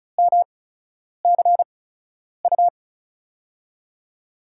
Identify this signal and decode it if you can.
Morse code, heard as a keyed tone, about 36 wpm, spaced as for 10 wpm, 700 Hz. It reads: MCU
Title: USA Call Signs - 6 Characters - 8x Character Spacing - 35wpm